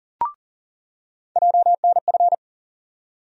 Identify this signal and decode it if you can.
Morse code, heard as a keyed tone, about 40 wpm, 700 Hz. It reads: JNF
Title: USA Call Sign Postfixes - 3 Letters 40wpm